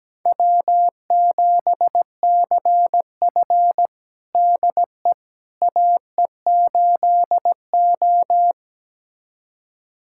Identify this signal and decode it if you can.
Morse code, heard as a keyed tone, about 17 wpm, 700 Hz. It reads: W7CF DE AE8O
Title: USA Callsign DE Callsign - 4 Characters - 17wpm